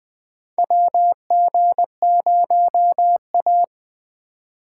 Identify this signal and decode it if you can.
Morse code, heard as a keyed tone, about 20 wpm, 700 Hz. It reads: WG0A
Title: USA Call Signs - 4 Characters - 3x Character Spacing - 20wpm